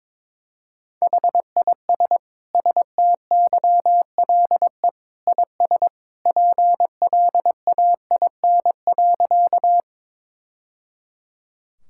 Morse code, heard as a keyed tone, about 22 wpm, 700 Hz. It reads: HIS STYLE IS PLAIN.
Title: Sentences from Top 1000 Words - Review Part 5 - 22wpm